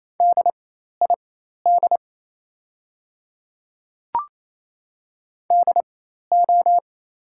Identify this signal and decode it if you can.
Morse code, heard as a keyed tone, about 28 wpm, spaced as for 13 wpm, 700 Hz. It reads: DID DO
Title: 4x Character Spacing - Top 100 Words - 28wpm